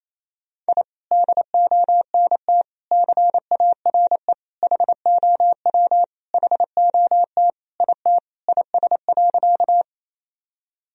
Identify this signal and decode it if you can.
Morse code, heard as a keyed tone, about 28 wpm, 700 Hz. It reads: I DONT CARE HOW HOT IT IS.
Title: Sentences from Top 1000 Words - Review Part 3 - 28wpm